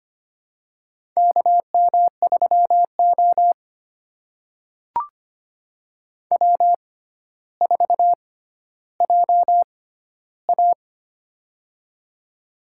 Morse code, heard as a keyed tone, about 25 wpm, spaced as for 13 wpm, 700 Hz. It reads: KM3O W4JA
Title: USA Call Signs - 4 Characters - 6x Character Spacing - 25wpm